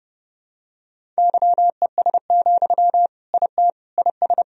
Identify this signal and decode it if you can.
Morse code, heard as a keyed tone, about 30 wpm, 700 Hz. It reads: YES, IT IS
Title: Sentences from Top 1000 Words - Review Part 1 - 30wpm